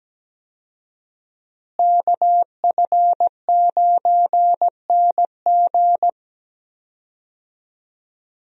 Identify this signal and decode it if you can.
Morse code, heard as a keyed tone, about 17 wpm, 700 Hz. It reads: KF9NG
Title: USA Call Signs - 5 Characters - Repeated 3x - 17wpm